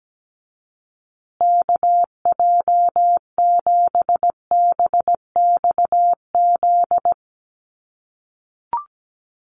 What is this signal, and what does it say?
Morse code, heard as a keyed tone, about 17 wpm, 700 Hz. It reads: KJ7BXZ
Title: USA Call Signs Repeated 3x 17wpm